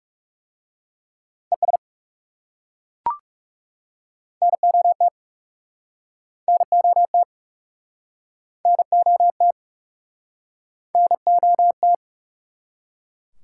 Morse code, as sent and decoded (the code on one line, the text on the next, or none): . ... / -. --- - / -. --- - / -. --- - / -. --- -
ES NOT NOT NOT NOT